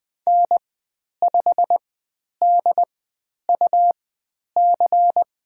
Morse code, heard as a keyed tone, about 20 wpm, spaced as for 10 wpm, 700 Hz. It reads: N5DUC
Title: 102 Lesson 17g - Letter u - Call Signs 20wpm